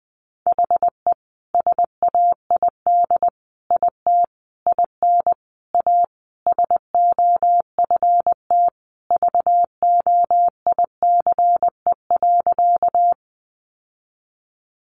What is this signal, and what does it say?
Morse code, heard as a keyed tone, about 19 wpm, 700 Hz. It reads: HE SAID IT IN A SOFT VOICE.
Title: Sentences from Top 1500 Words 20wpm